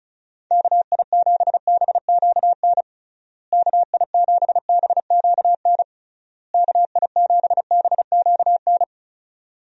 Morse code, heard as a keyed tone, about 36 wpm, 700 Hz. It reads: KI7BQN KI7BQN KI7BQN
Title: USA Call Signs - 6 Characters - Repeated 3x - 35wpm